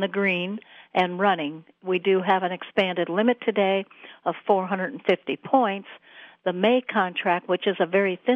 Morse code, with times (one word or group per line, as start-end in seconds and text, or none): none